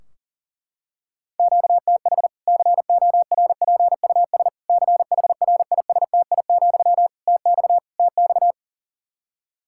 Morse code, heard as a keyed tone, about 40 wpm, 700 Hz. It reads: QTH CORPUS CHRISTI, TX TX
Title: K1USN SST QSO Practice 40wpm